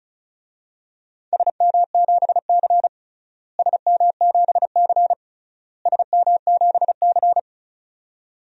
Morse code, heard as a keyed tone, about 36 wpm, 700 Hz. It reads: SM7C SM7C SM7C
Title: DXCC-100 Call Signs - 4 Characters - Repeated 3x - 35wpm